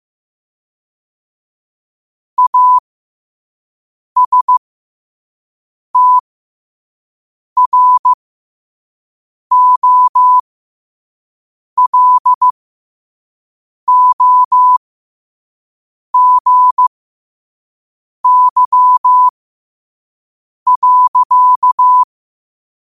Morse code, heard as a keyed tone, about 15 wpm, spaced as for 5 wpm, 1000 Hz. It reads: ASTROLOGY.
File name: qotdfiles / 5wpm